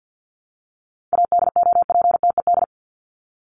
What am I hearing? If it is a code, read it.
Morse code, heard as a keyed tone, about 49 wpm, 700 Hz. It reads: ADOPTED
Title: ED Suffix - Top 100 Words with Base Words 50wpm